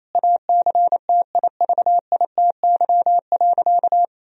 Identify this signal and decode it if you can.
Morse code, heard as a keyed tone, about 28 wpm, 700 Hz. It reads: ACTIVITY.
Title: Sentences from Top 1000 Words - Review Part 4 - 28wpm